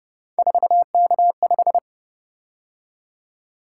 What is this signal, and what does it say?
Morse code, heard as a keyed tone, about 28 wpm, 700 Hz. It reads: 4K5